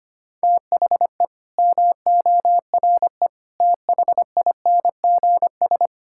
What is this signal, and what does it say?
Morse code, heard as a keyed tone, about 25 wpm, 700 Hz. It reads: THE MORE THINGS
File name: practice / 25wpm / Sentences with 9-Word Phrases - 25wpm